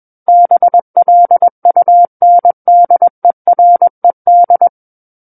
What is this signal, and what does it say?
Morse code, heard as a keyed tone, about 21 wpm, 700 Hz. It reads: BLUNDERED
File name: Book One / 12-Chapter 12 - 21 WPM